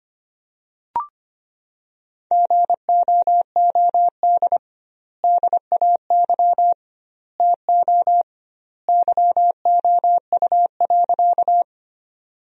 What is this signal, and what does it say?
Morse code, heard as a keyed tone, about 25 wpm, 700 Hz. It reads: GOOD DAY TO YOU.
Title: 2x Word-Spacing - Sentences from Top 200 Words - 25wpm